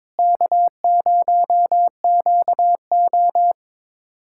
Morse code, heard as a keyed tone, about 21 wpm, 700 Hz. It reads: K0QO
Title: USA Callsign DE Callsign - 4 Characters - 22wpm